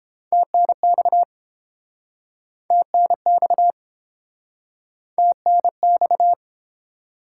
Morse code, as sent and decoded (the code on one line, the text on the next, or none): - -. -..- / - -. -..- / - -. -..-
TNX TNX TNX